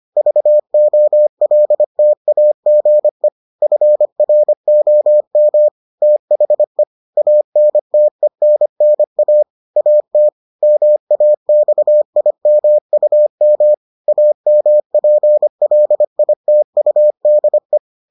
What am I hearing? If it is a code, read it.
Morse code, heard as a keyed tone, about 25 wpm, 600 Hz. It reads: VOLTAGE FROM THE ANTENNA AT MAXIMUM AMPLITUDE